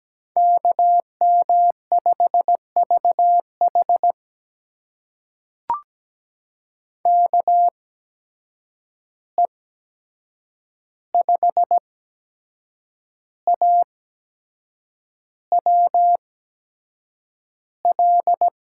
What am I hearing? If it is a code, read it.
Morse code, heard as a keyed tone, about 17 wpm, spaced as for 7 wpm, 700 Hz. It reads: KM5VH KE5AWL
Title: USA Call Signs - 8x Character Spacing - 17wpm